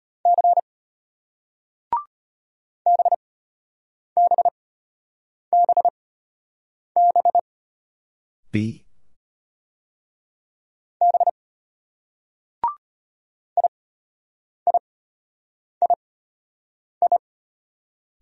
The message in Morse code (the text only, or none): C BBBB B IIII